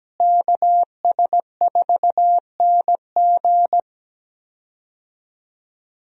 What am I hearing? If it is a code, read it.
Morse code, heard as a keyed tone, about 17 wpm, 700 Hz. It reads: KS4NG
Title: USA Call Signs - 5 Characters - Repeated 2x - 17wpm